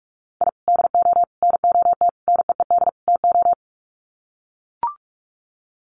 Morse code, heard as a keyed tone, about 44 wpm, 700 Hz. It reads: I DO NOT NEED TO
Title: Sets of 5 Words 45wpm